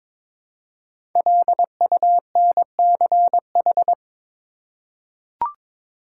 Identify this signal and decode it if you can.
Morse code, heard as a keyed tone, about 22 wpm, 700 Hz. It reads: LUNCH
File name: 5 Letter Words - First 3 Letters Given - 22wpm